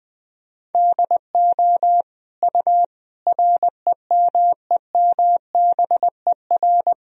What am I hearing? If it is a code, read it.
Morse code, heard as a keyed tone, about 21 wpm, 700 Hz. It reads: DO U REMEMBER
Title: Sentences from Top 400 Words and Abbreviations 20wpm